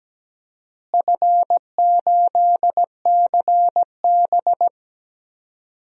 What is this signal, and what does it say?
Morse code, heard as a keyed tone, about 17 wpm, 700 Hz. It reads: F8CB
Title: DXCC-100 Call Signs - 4 Characters - Repeated 2x - 17wpm